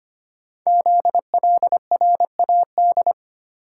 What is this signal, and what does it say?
Morse code, heard as a keyed tone, about 25 wpm, 700 Hz. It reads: ZLRAD
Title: Code Groups - 5 Letter:Numbers - 25wpm